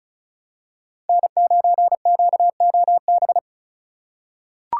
Morse code, heard as a keyed tone, about 36 wpm, 700 Hz. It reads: N9QOB
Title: USA Call Signs - 5 Characters - 3x Character Spacing - 35wpm